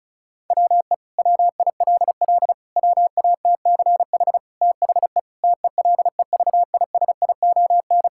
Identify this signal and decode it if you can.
Morse code, heard as a keyed tone, about 34 wpm, 700 Hz. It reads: WE WILL WATCH THE TELEVISION
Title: Sentences from Top 1000 Words - Review Part 5 - 35wpm